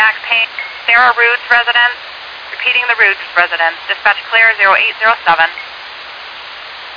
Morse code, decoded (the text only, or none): none